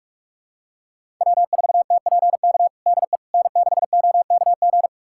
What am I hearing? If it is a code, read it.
Morse code, heard as a keyed tone, about 46 wpm, 700 Hz. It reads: W4TPK DE N6OKG
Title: USA Callsign DE Callsign - 5 Characters - 45wpm